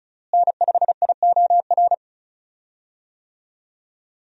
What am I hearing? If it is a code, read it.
Morse code, heard as a keyed tone, about 34 wpm, 700 Hz. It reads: N5IOR
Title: USA Call Signs - 5 Characters - Repeated 3x - 35wpm